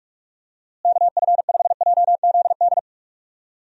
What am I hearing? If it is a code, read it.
Morse code, heard as a keyed tone, about 46 wpm, 700 Hz. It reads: KF5JZD
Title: USA Call Signs - 6 Characters - Repeated 2x - 45wpm